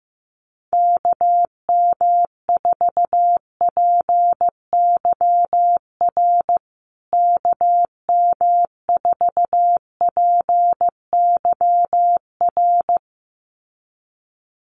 Morse code, heard as a keyed tone, about 15 wpm, 700 Hz. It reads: KM4PYR KM4PYR
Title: USA Call Signs Repeated 2x 15wpm